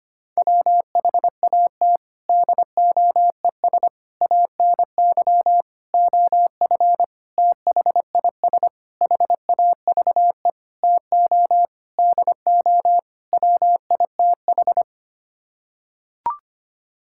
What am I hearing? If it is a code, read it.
Morse code, heard as a keyed tone, about 25 wpm, 700 Hz. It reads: WHAT DOES ANY OF THIS HAVE TO DO WITH
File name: Sets of 9 Words - 25wpm